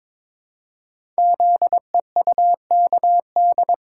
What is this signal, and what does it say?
Morse code, heard as a keyed tone, about 22 wpm, 700 Hz. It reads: ZEUKD
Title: Code Groups - 5 Letters - 22wpm